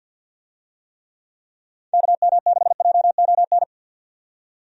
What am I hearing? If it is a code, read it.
Morse code, heard as a keyed tone, about 50 wpm, 700 Hz. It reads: KM6JON
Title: USA Call Signs - 6 Characters - 50wpm